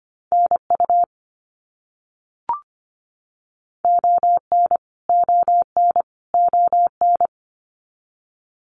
Morse code, heard as a keyed tone, about 25 wpm, 700 Hz. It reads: NU ON ON ON
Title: Canadian Provinces and Territories - Repeated 3x - 25wpm